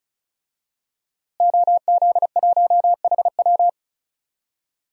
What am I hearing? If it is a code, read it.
Morse code, heard as a keyed tone, about 36 wpm, 700 Hz. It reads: OZ1HW
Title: DXCC-100 Call Signs - 5 Characters - Repeated 3x - 35wpm